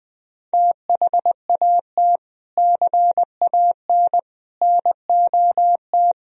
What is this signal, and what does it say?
Morse code, heard as a keyed tone, about 20 wpm, 700 Hz. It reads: THAT CAN NOT
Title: Sentences from Top 1000 Words - Review Part 2 - 20wpm